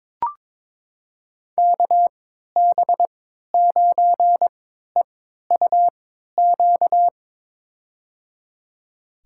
Morse code, heard as a keyed tone, about 22 wpm, spaced as for 13 wpm, 700 Hz. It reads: KB9EUQ
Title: USA Call Signs - 6 Characters - 3x Character Spacing - 22wpm